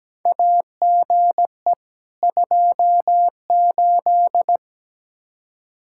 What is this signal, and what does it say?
Morse code, heard as a keyed tone, about 17 wpm, 700 Hz. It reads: AGE 28
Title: Practice QSO - Exchange 3 - More Info - 17wpm